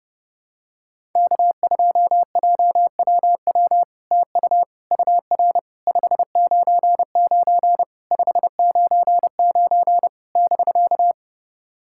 Morse code, heard as a keyed tone, about 30 wpm, 700 Hz. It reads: K2JWW TU UR 599 599 <BK>
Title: SOTA QSO Practice 30wpm